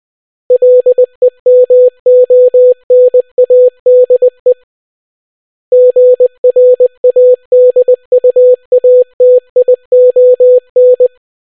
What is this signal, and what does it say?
Morse code, heard as a keyed tone, about 20 wpm, 500 Hz. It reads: LEMONADE GRADUATION